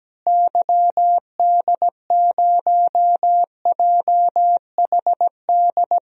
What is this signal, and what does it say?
Morse code, heard as a keyed tone, about 17 wpm, 700 Hz. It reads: YD0JHD
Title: DXCC-100 Call Signs - 6 Characters - Repeated 2x - 17wpm